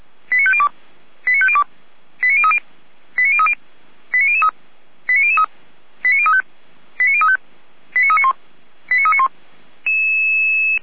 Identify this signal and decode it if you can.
Five-tone selective call (ZVEI), heard as quick runs of five tones, tones of about 70 ms. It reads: 89502 89502 89030 89030 89003 89003 89035 89035 89391 89391
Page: SDR/ZVEI